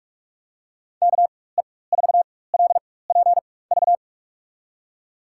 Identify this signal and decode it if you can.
Morse code, heard as a keyed tone, about 44 wpm, spaced as for 21 wpm, 700 Hz. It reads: KE4LPV
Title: USA Call Signs - 6 Characters - 4x Character Spacing - 45wpm